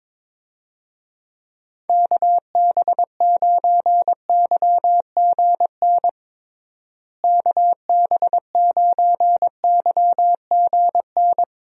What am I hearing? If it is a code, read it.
Morse code, heard as a keyed tone, about 22 wpm, 700 Hz. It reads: KB9YGN KB9YGN
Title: USA Call Signs - 6 Characters - Repeated 3x - 22wpm